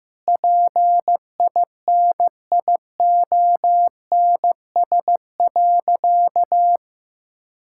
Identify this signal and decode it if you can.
Morse code, heard as a keyed tone, about 16 wpm, 700 Hz. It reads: PINIONS.
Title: Pangram Challenge v2 - 15wpm